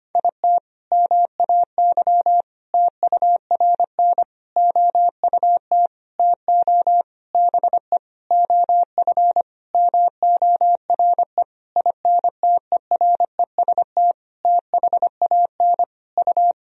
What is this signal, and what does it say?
Morse code, heard as a keyed tone, about 25 wpm, 700 Hz. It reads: IT MAY TURN OUT TO BE OF MORE INTEREST THAN U
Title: Sentences from Top 400 Words and Abbreviations 25wpm